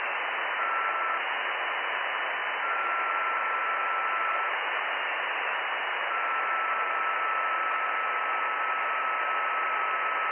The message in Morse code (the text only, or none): none